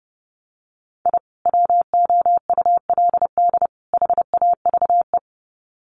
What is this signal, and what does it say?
Morse code, heard as a keyed tone, about 30 wpm, 700 Hz. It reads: I WOULD HAVE
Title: Sets of 3 Words 30wpm